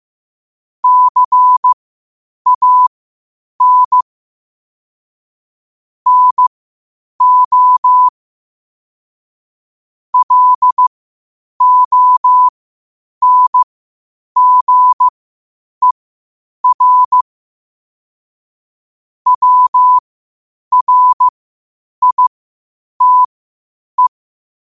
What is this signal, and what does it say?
Morse code, heard as a keyed tone, about 15 wpm, spaced as for 8 wpm, 1000 Hz. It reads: CAN NO LONGER WRITE
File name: qotdfiles / 8wpm